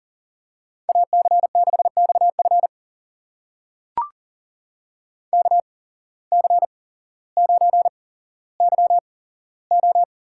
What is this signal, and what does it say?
Morse code, heard as a keyed tone, about 41 wpm, spaced as for 17 wpm, 700 Hz. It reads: AC6XF KC9YO
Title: USA Call Signs - 8x Character Spacing - 40wpm